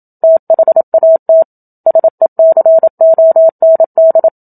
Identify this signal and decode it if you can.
Morse code, heard as a keyed tone, about 27 wpm, 650 Hz. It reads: THAT SECOND
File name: Book Two / 01-Chapter 1 - 27 WPM